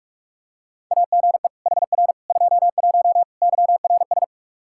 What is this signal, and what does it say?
Morse code, heard as a keyed tone, about 46 wpm, 700 Hz. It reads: AGE HR 21 YRS
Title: K1USN SST QSO Practice 45wpm